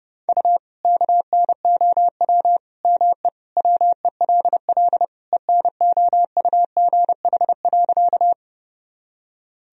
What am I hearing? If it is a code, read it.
Morse code, heard as a keyed tone, about 30 wpm, 700 Hz. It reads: U KNOW ME WELL ENOUGH.
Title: Sentences from Top 400 Words and Abbreviations 30wpm